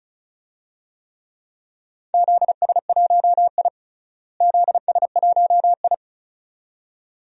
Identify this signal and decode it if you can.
Morse code, heard as a keyed tone, about 36 wpm, 700 Hz. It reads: ZS1I ZS1I
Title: DXCC-100 Call Signs - 4 Characters - Repeated 2x - 35wpm